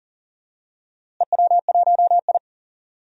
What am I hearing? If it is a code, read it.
Morse code, heard as a keyed tone, about 38 wpm, 700 Hz. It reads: EW1I
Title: DXCC-100 Call Signs - 4 Characters - Repeated 2x - 40wpm